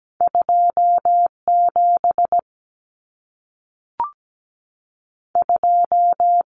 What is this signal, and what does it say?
Morse code, heard as a keyed tone, about 17 wpm, 700 Hz. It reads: 2 7 2
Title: Numbers - 2-Digit 17wpm